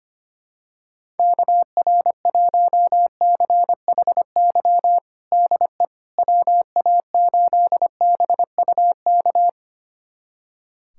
Morse code, heard as a keyed tone, about 25 wpm, 700 Hz. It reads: KR1CHY DE WA8BUK
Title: USA Callsign DE Callsign - 6 Characters - 25wpm